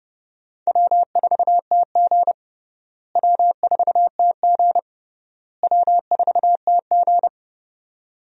Morse code, heard as a keyed tone, about 32 wpm, 700 Hz. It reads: W4TG W4TG W4TG
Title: USA Call Signs - 4 Characters - Repeated 3x - 30wpm